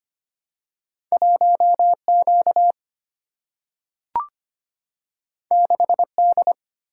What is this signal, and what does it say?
Morse code, heard as a keyed tone, about 25 wpm, 700 Hz. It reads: 1 Q 6 D
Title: Single-Letter Single-Number 25wpm